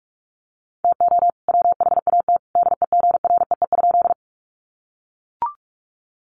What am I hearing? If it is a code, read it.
Morse code, heard as a keyed tone, about 44 wpm, 700 Hz. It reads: TO WHAT DEGREE?
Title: Sentences from Top 1500 Words 45wpm